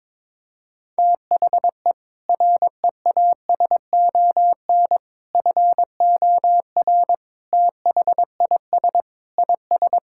Morse code, heard as a keyed tone, about 22 wpm, 700 Hz. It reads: THE REASON FOR THIS IS
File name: practice / 22wpm / Sets of 7 Words - 22wpm